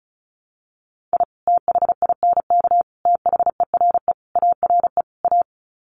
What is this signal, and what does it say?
Morse code, heard as a keyed tone, about 34 wpm, 700 Hz. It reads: I THINK THERE ARE A
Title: Sets of 5 Words Encore2 35wpm